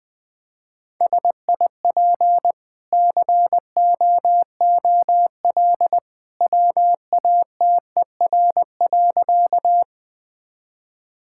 Morse code, heard as a keyed tone, about 20 wpm, 700 Hz. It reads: SIP COOL WATER.
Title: Easy Sentences with Hints - 20wpm